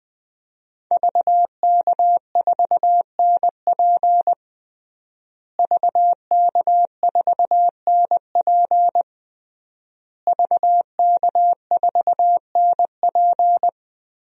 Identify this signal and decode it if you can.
Morse code, heard as a keyed tone, about 20 wpm, 700 Hz. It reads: VK4NP VK4NP VK4NP
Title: DXCC-100 Call Signs - 5 Characters - Repeated 3x - 20wpm